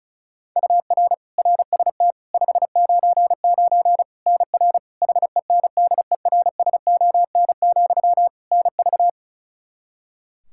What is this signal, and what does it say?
Morse code, heard as a keyed tone, about 36 wpm, 700 Hz. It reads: UR RST 599 NR HENDERSON, NV
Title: Practice QSO - Exchange 1 - Beginning - 35wpm